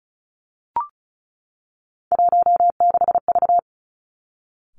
Morse code, heard as a keyed tone, about 36 wpm, 700 Hz. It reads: 16V
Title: Three Challenging Letter-Number Combinations 35wpm